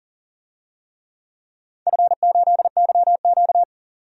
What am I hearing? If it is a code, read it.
Morse code, heard as a keyed tone, about 37 wpm, 700 Hz. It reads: F8YQ
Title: DXCC-100 Call Signs - 4 Characters - 40wpm